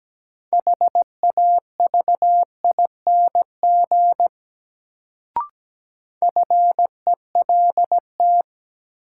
Morse code, heard as a keyed tone, about 17 wpm, 700 Hz. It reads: HAVING FELT
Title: Sets of 5 Words Encore - New Word Drill - 17wpm